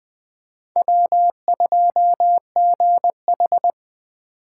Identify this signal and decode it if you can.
Morse code, heard as a keyed tone, about 24 wpm, 700 Hz. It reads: W2GH
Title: Speed Racing - USA 1x2 Call Signs 20wpm